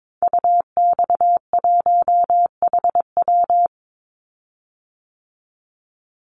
Morse code, heard as a keyed tone, about 22 wpm, 700 Hz. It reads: UX1HW
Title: Call Signs Repeated 3x - 22wpm